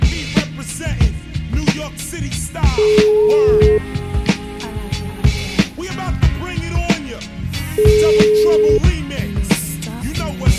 Ringback tone (Europe, 425 Hz), with noise.